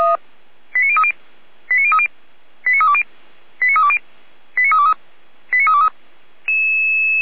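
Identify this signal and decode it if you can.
Five-tone selective call (ZVEI), heard as quick runs of five tones, tones of about 70 ms. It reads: 89030 89030 89320 89320 89323 89323
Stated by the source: ZVEI five-tone selective call received on SDR.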